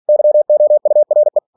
Morse code, heard as a keyed tone, about 46 wpm, 600 Hz. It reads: YOURE
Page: Never 47wpm